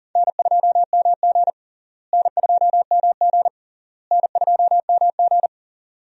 Morse code, heard as a keyed tone, about 40 wpm, 700 Hz. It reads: N2MG N2MG N2MG
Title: USA Call Signs - 4 Characters - Repeated 3x - 40wpm